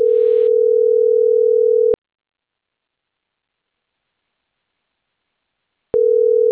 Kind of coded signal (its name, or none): telephone call progress tone